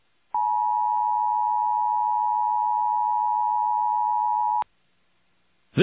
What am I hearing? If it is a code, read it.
Emergency Alert System (SAME) tones (the 853 and 960 Hz attention tone). It is heard from 0.3 to 4.7 s.